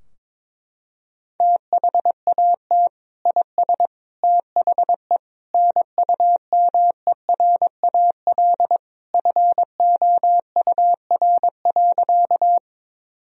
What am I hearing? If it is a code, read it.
Morse code, heard as a keyed tone, about 22 wpm, 700 Hz. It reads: THAT IS THE NUMERAL FOUR.
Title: Sentences from Top 1000 Words - Review Part 2 - 22wpm